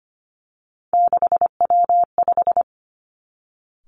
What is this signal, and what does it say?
Morse code, heard as a keyed tone, about 25 wpm, 700 Hz. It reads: 6W5
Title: Three Challenging Letter-Number Combinations 25wpm